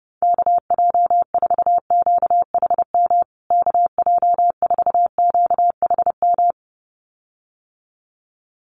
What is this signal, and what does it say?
Morse code, heard as a keyed tone, about 30 wpm, 700 Hz. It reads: KJ4QHM KJ4QHM
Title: USA Call Signs Repeated 2x 30wpm